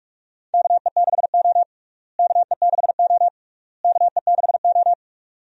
Morse code, heard as a keyed tone, about 45 wpm, 700 Hz. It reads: KE6O KE6O KE6O
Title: USA Call Signs - 4 Characters - Repeated 3x - 45wpm